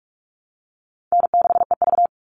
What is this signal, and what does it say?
Morse code, heard as a keyed tone, about 44 wpm, 700 Hz. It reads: N6EV
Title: Easy Call Signs - 45wpm